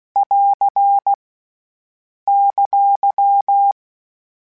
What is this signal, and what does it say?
Morse code, heard as a keyed tone, about 16 wpm, 800 Hz. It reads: <AR> !